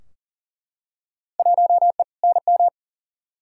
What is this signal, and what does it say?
Morse code, heard as a keyed tone, about 40 wpm, 700 Hz. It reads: 1E NM
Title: Field Day Exchanges - 40wpm